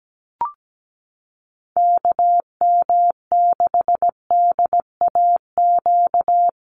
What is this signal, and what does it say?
Morse code, heard as a keyed tone, about 17 wpm, 700 Hz. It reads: KM6DAQ
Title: USA Call Signs Repeated 3x 17wpm